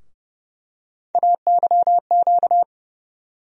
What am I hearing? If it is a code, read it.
Morse code, heard as a keyed tone, about 29 wpm, 700 Hz. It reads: AYQ
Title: USA Call Sign Postfixes - 3 Letters 30wpm